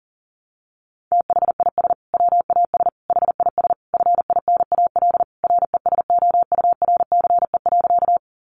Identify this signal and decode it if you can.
Morse code, heard as a keyed tone, about 40 wpm, 700 Hz. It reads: THIS WAS HIS FINAL RESOURCE.
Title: Sentences from Top 1300 Words 40wpm